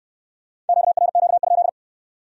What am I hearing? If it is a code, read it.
Morse code, heard as a keyed tone, about 59 wpm, 700 Hz. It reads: QRQ?